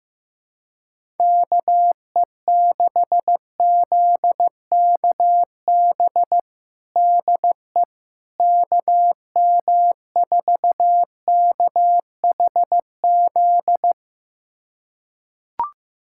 Morse code, heard as a keyed tone, about 16 wpm, 700 Hz. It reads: KE6ZKB DE KM4KHZ
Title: USA Callsign DE Callsign 15wpm